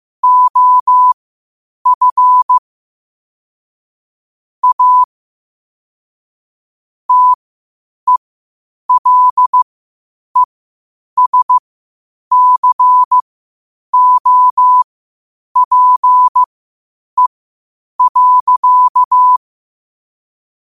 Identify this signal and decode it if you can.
Morse code, heard as a keyed tone, about 15 wpm, spaced as for 8 wpm, 1000 Hz. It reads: OF A TELESCOPE.
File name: qotdfiles / 8wpm